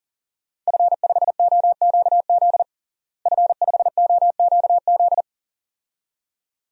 Morse code, heard as a keyed tone, about 37 wpm, 700 Hz. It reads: F5OQZ F5OQZ
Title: DXCC-100 Call Signs - 5 Characters - Repeated 2x - 40wpm